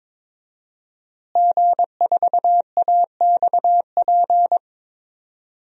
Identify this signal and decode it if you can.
Morse code, heard as a keyed tone, about 22 wpm, 700 Hz. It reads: G4AXP
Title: DXCC-100 Call Signs - 5 Characters - Repeated 2x - 22wpm